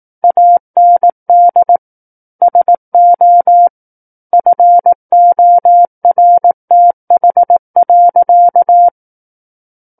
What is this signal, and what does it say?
Morse code, heard as a keyed tone, about 18 wpm, 700 Hz. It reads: AND SO FORTH.
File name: 18 WPM